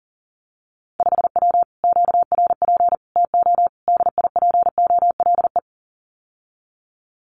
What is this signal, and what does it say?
Morse code, heard as a keyed tone, about 38 wpm, 700 Hz. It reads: 5W QRP TO DIPOLE
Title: Common QSO Phrases Encore 40wpm